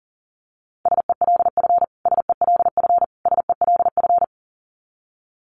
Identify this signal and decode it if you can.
Morse code, heard as a keyed tone, about 40 wpm, 700 Hz. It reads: SELF SELF SELF
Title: Top 500 Words Repeated 40wpm